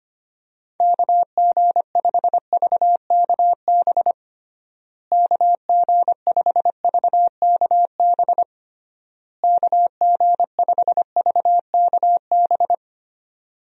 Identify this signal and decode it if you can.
Morse code, heard as a keyed tone, about 25 wpm, 700 Hz. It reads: KG5VKB KG5VKB KG5VKB
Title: USA Call Signs - 6 Characters - Repeated 3x - 25wpm